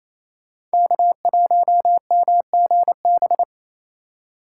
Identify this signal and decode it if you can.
Morse code, heard as a keyed tone, about 37 wpm, 700 Hz. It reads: K1MGB